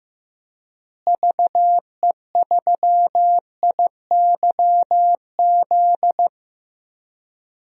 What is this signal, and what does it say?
Morse code, heard as a keyed tone, about 15 wpm, 700 Hz. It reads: VE3IYZ
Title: DXCC-100 Call Signs - 6 Characters - Repeated 3x - 15wpm